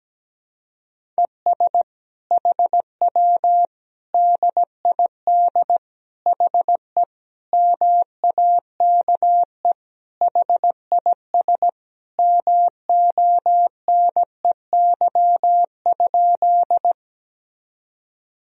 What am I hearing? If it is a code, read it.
Morse code, heard as a keyed tone, about 17 wpm, 700 Hz. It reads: ES HW DID HE MAKE HIS MONEY?
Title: Sentences from Top 400 Words and Abbreviations 17wpm